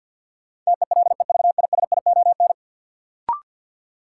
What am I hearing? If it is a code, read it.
Morse code, heard as a keyed tone, about 47 wpm, 700 Hz. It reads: TELEVISION
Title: Sentences from Top 1000 Words - Review Part 4 - New Word Drill - 50wpm